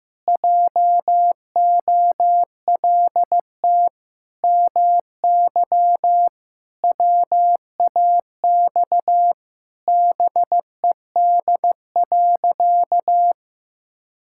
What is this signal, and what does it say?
Morse code, heard as a keyed tone, about 16 wpm, 700 Hz. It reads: JOLT MY WAX BED.
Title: Pangram Challenge v2 - 15wpm